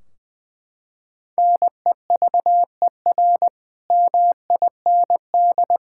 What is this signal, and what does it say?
Morse code, heard as a keyed tone, about 21 wpm, 700 Hz. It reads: NEVER MIND
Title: Sentences from Top 1600 Words 20wpm